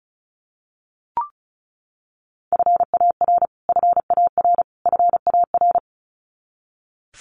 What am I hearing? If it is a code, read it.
Morse code, heard as a keyed tone, about 34 wpm, 700 Hz. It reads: FAR FAR FAR
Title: Top 300 Words Repeated 35wpm